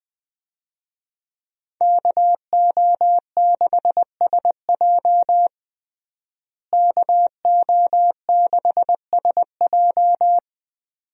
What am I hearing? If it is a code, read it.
Morse code, heard as a keyed tone, about 20 wpm, 700 Hz. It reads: KO6SJ KO6SJ
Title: USA Call Signs - 5 Characters - Repeated 2x - 20wpm